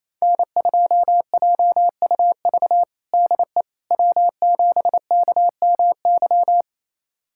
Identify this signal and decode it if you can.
Morse code, heard as a keyed tone, about 27 wpm, 700 Hz. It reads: N2JUV DE W7KMY
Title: USA Callsign DE Callsign - 5 Characters - 28wpm